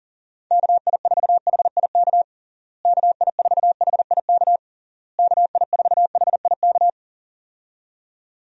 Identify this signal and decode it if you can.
Morse code, heard as a keyed tone, about 40 wpm, 700 Hz. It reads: KI4HIK KI4HIK KI4HIK
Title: USA Call Signs - 6 Characters - Repeated 3x - 40wpm